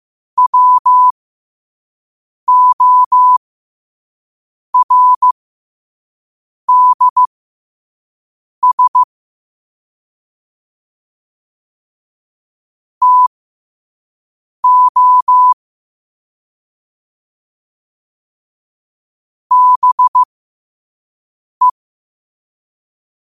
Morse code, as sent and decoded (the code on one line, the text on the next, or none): .-- --- .-. -.. ... / - --- / -... .
WORDS TO BE